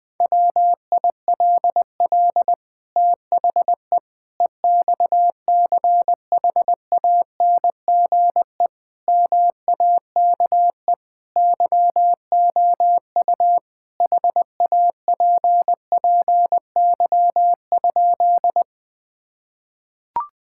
Morse code, heard as a keyed tone, about 20 wpm, 700 Hz. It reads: WILL THE EXCHANGE MAKE YOU HAPPY?
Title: Sentences from Top 1000 Words - Review Part 6 - 20wpm